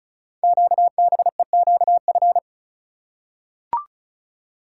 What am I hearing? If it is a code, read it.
Morse code, heard as a keyed tone, about 36 wpm, 700 Hz. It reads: QBEQF
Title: Code Groups - 5 Letters - 35wpm